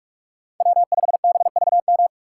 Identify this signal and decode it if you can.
Morse code, heard as a keyed tone, about 43 wpm, 700 Hz. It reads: W5BVM